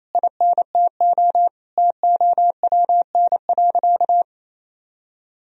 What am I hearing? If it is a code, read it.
Morse code, heard as a keyed tone, about 28 wpm, 700 Hz. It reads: INTO TOWN.
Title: Sentences from Top 1000 Words - Review Part 2 - 28wpm